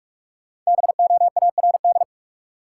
Morse code, heard as a keyed tone, about 43 wpm, 700 Hz. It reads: BOARD
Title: Sentences from Top 700 Words - New Word Drill - 45wpm